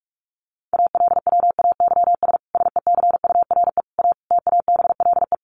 Morse code, heard as a keyed tone, about 44 wpm, 700 Hz. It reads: ALWAYS SECURE A TABLE
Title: Sentences from Top 1200 Words 45wpm